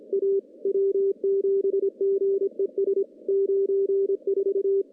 Morse code, heard as a keyed tone, about 26 wpm, 390 Hz. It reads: A W7GES 94
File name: contests / CW-SS-k5tr- -1300